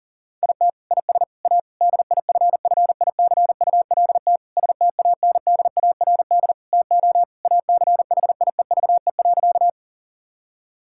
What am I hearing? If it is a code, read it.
Morse code, heard as a keyed tone, about 40 wpm, 700 Hz. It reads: IT IS A DIFFICULT STANDARD TO ACHIEVE.
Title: Sentences from Top 1000 Words - Review Part 6 - 40wpm